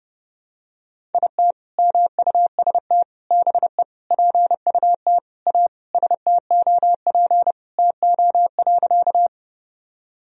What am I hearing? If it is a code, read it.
Morse code, heard as a keyed tone, about 30 wpm, 700 Hz. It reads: IT MUST BE PUT A STOP TO.
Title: Sentences from Top 1000 Words - Review Part 1 - 30wpm